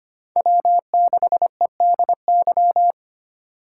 Morse code, heard as a keyed tone, about 25 wpm, 700 Hz. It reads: W6EDY
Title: K1USN SST QSO Practice - 25wpm